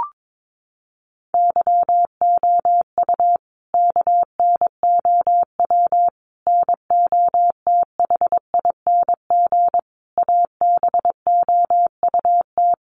Morse code, heard as a keyed tone, about 22 wpm, 700 Hz. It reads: YOU KNOW NOTHING ABOUT
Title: Sentences from Top 500 Words 22wpm